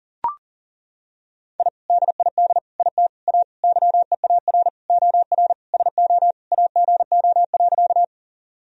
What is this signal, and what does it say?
Morse code, heard as a keyed tone, about 37 wpm, 700 Hz. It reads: I DID IT A YEAR OR SO AGO.
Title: Sentences from Top 1000 Words - Review Part 3 - 40wpm